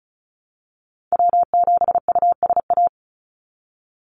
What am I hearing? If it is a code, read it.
Morse code, heard as a keyed tone, about 34 wpm, 700 Hz. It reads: W7USA
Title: NA SOTA Chaser Callsigns 35wpm